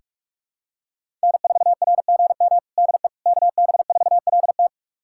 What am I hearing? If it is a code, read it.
Morse code, heard as a keyed tone, about 43 wpm, 700 Hz. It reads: N4RGM DE KB4LT
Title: USA Callsign DE Callsign - 5 Characters - 45wpm